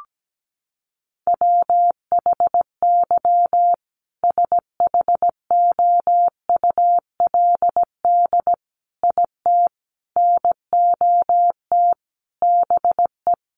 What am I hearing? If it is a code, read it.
Morse code, heard as a keyed tone, about 17 wpm, 700 Hz. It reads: WHY SHOULD IT NOT BE
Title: Sentences from Top 1400 Words 17wpm